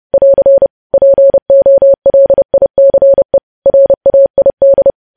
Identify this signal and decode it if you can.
Morse code, heard as a keyed tone, about 30 wpm, 550 Hz. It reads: <AR> POLICE RAID
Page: news headlines in morse code at 30 wpm